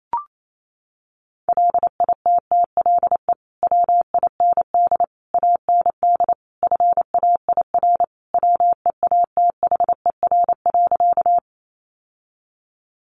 Morse code, heard as a keyed tone, about 28 wpm, 700 Hz. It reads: LITTLE WIND AND FAIR WEATHER.
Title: Sentences from Top 1300 Words 28wpm